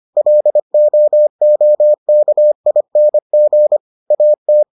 Morse code, heard as a keyed tone, about 25 wpm, 600 Hz. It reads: LOOKING AT